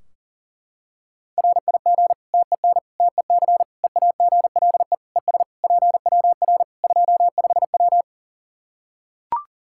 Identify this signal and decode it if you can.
Morse code, heard as a keyed tone, about 40 wpm, 700 Hz. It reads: RIG TEN TEC EAGLE ES PWR 25W
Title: Practice QSO - Exchange 2 - Info - 40wpm